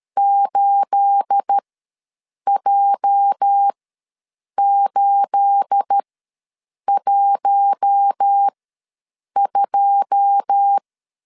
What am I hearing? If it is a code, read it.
Morse code, heard as a keyed tone, about 13 wpm, spaced as for 7 wpm, 800 Hz. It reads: 8J812
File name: LEZIONI / 9 28 ws 12wpm 3354